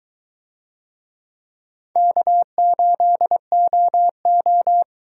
Morse code, heard as a keyed tone, about 21 wpm, 700 Hz. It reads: K8OO